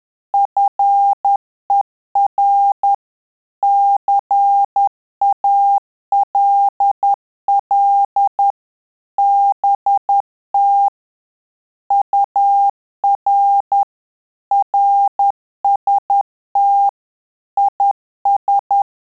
Morse code, heard as a keyed tone, about 10 wpm, 800 Hz. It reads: FER CALL BT UR RST IS
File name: Telegraphy-group / QSO CW 50 CPM